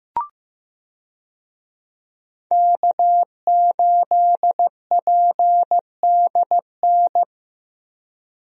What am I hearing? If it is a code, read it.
Morse code, heard as a keyed tone, about 15 wpm, 700 Hz. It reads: K8PDN
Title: USA Call Signs - 5 Characters - Repeated 2x - 15wpm